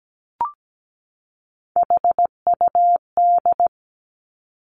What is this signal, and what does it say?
Morse code, heard as a keyed tone, about 17 wpm, 700 Hz. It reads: HUD
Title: Three Challenging Letter-Number Combinations 17wpm